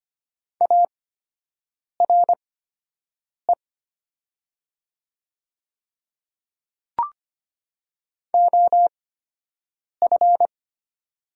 Morse code, heard as a keyed tone, about 25 wpm, spaced as for 7 wpm, 700 Hz. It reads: ARE OF
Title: 8x Character Spacing - Top 100 Words - 25wpm